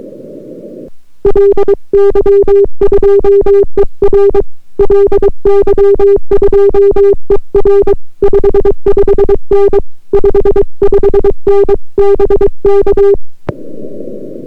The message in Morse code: LY2ER LY2ER 55N 55N BK